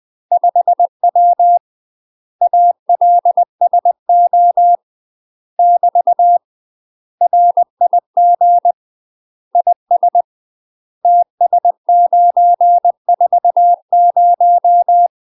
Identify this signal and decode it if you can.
Morse code, heard as a keyed tone, about 20 wpm, 700 Hz. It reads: 5W ALSO <BT> RIG IS TS940
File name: UK MASTER QSOs TRAINING / 20WPM / UK MASTER QSOs 0067